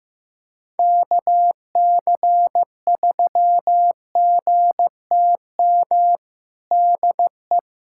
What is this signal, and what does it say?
Morse code, heard as a keyed tone, about 15 wpm, 700 Hz. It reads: KC3GTM DE
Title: USA Callsign DE Callsign - 6 Characters - 15wpm